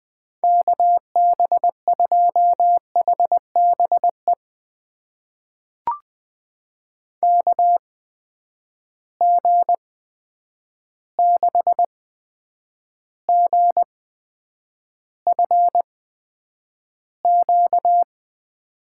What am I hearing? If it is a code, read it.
Morse code, heard as a keyed tone, about 20 wpm, spaced as for 9 wpm, 700 Hz. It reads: KB2HBE KG6GFQ
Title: USA Call Signs - 6 Characters - 8x Character Spacing - 20wpm